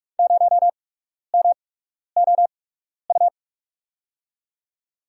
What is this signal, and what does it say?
Morse code, heard as a keyed tone, about 41 wpm, spaced as for 12 wpm, 700 Hz. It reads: 0MOU
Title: USA Call Signs - 6 Characters - 8x Character Spacing - 45wpm